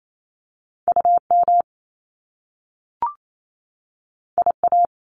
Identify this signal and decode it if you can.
Morse code, heard as a keyed tone, about 28 wpm, 700 Hz. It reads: UM IA
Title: Top Two Letter Combinations in Words 28wpm